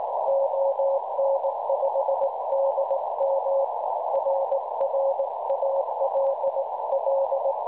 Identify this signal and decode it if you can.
Morse code, heard as a keyed tone, about 18 wpm, 550 Hz, with noise. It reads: ON5DM RR ALL
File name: cwronde / websdr recording .6kHz